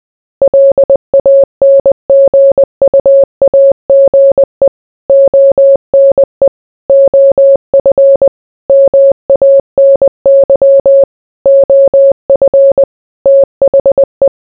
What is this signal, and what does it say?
Morse code, heard as a keyed tone, about 20 wpm, 550 Hz. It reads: LANGUAGE ONE OF MANY OF THE